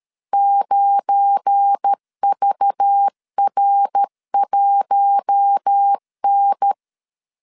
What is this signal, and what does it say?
Morse code, heard as a keyed tone, about 13 wpm, 800 Hz. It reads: 9VR1N
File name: CW / DET / 1 9 1500C S B 12wpm 3819